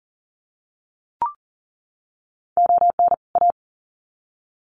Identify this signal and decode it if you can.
Morse code, heard as a keyed tone, about 40 wpm, 700 Hz. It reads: ON A
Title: Sets of 2 Words 40wpm